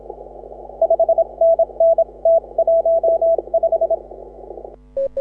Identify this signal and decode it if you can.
Morse code, heard as a keyed tone, about 26 wpm, 660 Hz. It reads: 5NN T15
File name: AUDIO2018 / HA6OI CW